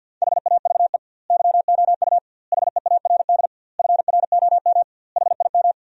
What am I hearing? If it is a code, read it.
Morse code, heard as a keyed tone, about 50 wpm, 700 Hz. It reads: HAVE YOU HEARD FROM HIM
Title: 2x Word-Spacing - Sentences from Top 400 Words - 50wpm